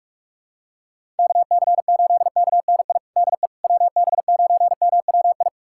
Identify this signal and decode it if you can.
Morse code, heard as a keyed tone, about 46 wpm, 690 Hz. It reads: KC8KNI DE WB9MWI
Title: USA Callsign DE Callsign 45wpm